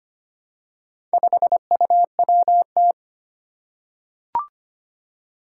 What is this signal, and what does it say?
Morse code, heard as a keyed tone, about 25 wpm, 700 Hz. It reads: 5UWT
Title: Four Challenging Letter-Number Combinations 25wpm